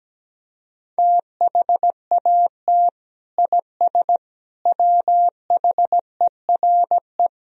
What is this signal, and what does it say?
Morse code, heard as a keyed tone, about 17 wpm, 700 Hz. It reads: THAT IS WHERE
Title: Sentences from Top 1000 Words - Review Part 1 - 17wpm